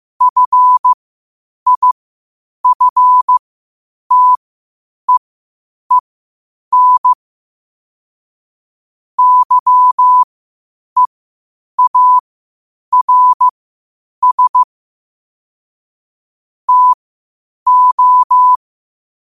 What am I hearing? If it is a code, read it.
Morse code, heard as a keyed tone, about 15 wpm, spaced as for 8 wpm, 1000 Hz. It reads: FIFTEEN YEARS TO
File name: qotdfiles / 8wpm